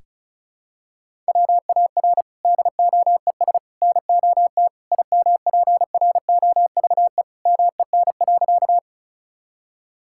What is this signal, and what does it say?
Morse code, heard as a keyed tone, about 36 wpm, 700 Hz. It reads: WAR DOES NOT IMPROVE MEN.
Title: Sentences from Top 1000 Words - Review Part 5 - 35wpm